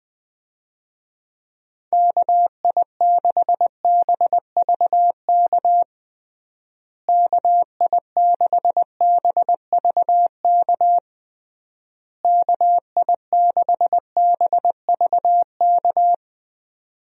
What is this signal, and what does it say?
Morse code, heard as a keyed tone, about 20 wpm, 700 Hz. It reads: KI6BVK KI6BVK KI6BVK
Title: USA Call Signs - 6 Characters - Repeated 3x - 20wpm